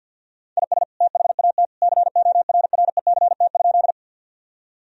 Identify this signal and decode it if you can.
Morse code, heard as a keyed tone, about 51 wpm, 700 Hz. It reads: IS THAT CORRECT?
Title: Sentences from Top 1000 Words - Review Part 2 - 50wpm